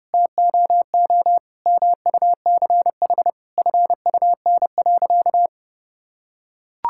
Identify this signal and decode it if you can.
Morse code, heard as a keyed tone, about 32 wpm, 700 Hz. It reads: TOO MUCH FUN.
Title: Sentences from Top 1600 Words 30wpm